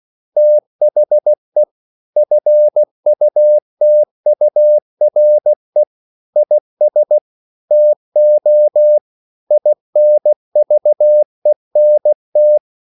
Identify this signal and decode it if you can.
Morse code, heard as a keyed tone, about 16 wpm, 600 Hz. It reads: THE FUTURE IS TO INVENT
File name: CW / Practice / EnglishQuotations0416